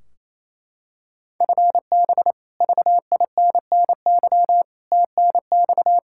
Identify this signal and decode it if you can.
Morse code, heard as a keyed tone, about 28 wpm, 700 Hz. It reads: FB VINNY TNX
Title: Practice QSO - Exchange 3 - More Info - 28wpm